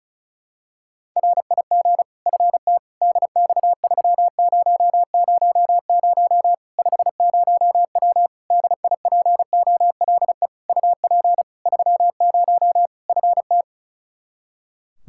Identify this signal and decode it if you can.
Morse code, heard as a keyed tone, about 36 wpm, 700 Hz. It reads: RIG FT DX3000 50W DIPOLE UP 30 FT
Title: Practice QSO - Exchange 2 - Info - 35wpm